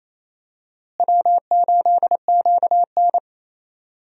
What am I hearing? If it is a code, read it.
Morse code, heard as a keyed tone, about 28 wpm, 700 Hz. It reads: W8QN
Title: USA Call Signs - 4 Characters - Repeated 3x - 28wpm